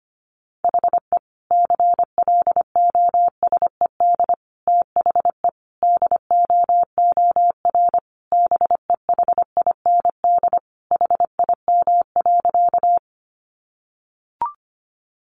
Morse code, heard as a keyed tone, about 25 wpm, 700 Hz. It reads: HE CLOSED THE DOOR BEHIND HIM.
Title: Sentences from Top 1400 Words 25wpm